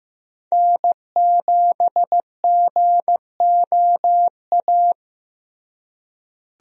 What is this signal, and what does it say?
Morse code, heard as a keyed tone, about 15 wpm, 700 Hz. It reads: N7GOA
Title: USA Call Signs - 5 Characters - Repeated 2x - 15wpm